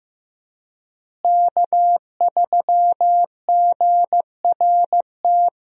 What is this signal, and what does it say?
Morse code, heard as a keyed tone, about 15 wpm, 700 Hz. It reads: K3GRT
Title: USA Call Signs - 5 Characters - Repeated 3x - 15wpm